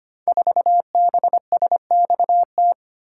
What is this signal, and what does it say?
Morse code, heard as a keyed tone, about 25 wpm, 700 Hz. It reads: 4BSXT
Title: Code Groups - 5 Letter:Numbers - 25wpm